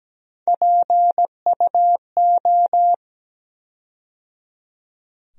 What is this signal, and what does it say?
Morse code, heard as a keyed tone, about 17 wpm, 700 Hz. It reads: PUO
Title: USA Call Sign Postfixes - 3 Letters 17wpm